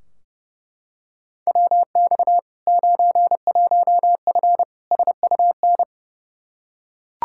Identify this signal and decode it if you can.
Morse code, heard as a keyed tone, about 32 wpm, 700 Hz. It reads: WX 91F SUN
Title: Practice QSO - Exchange 2 - Info - 30wpm